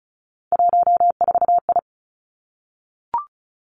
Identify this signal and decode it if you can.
Morse code, heard as a keyed tone, about 36 wpm, 700 Hz. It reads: 14I
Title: Three Challenging Letter-Number Combinations 35wpm